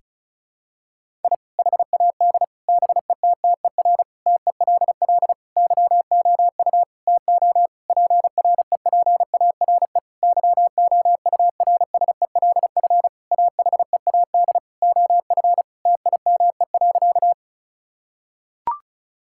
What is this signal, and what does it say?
Morse code, heard as a keyed tone, about 34 wpm, 700 Hz. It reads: I HAD BETTER TELL YOU TO PREPARE YOURSELF AHEAD OF TIME.
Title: Sentences from Top 1000 Words - Review Part 6 - 35wpm